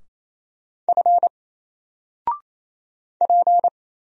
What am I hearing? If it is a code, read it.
Morse code, heard as a keyed tone, about 27 wpm, 700 Hz. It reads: F P
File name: Single-Letter-Number-28wpm